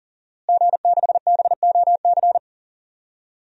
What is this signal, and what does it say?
Morse code, heard as a keyed tone, about 40 wpm, 700 Hz. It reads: G6BOC
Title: DXCC-100 Call Signs - 5 Characters - Repeated 3x - 40wpm